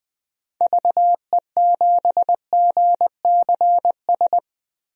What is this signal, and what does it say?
Morse code, heard as a keyed tone, about 21 wpm, 700 Hz. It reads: VE7GCS
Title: DXCC-100 Call Signs - 6 Characters - 20wpm